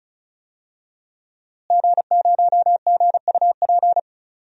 Morse code, heard as a keyed tone, about 36 wpm, 700 Hz. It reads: G0GUP
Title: DXCC-100 Call Signs - 5 Characters - Repeated 3x - 35wpm